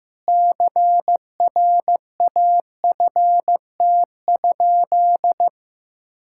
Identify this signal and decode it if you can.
Morse code, heard as a keyed tone, about 15 wpm, 700 Hz. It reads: CRAFT?
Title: Sentences from Top 1000 Words - Review Part 6 - 15wpm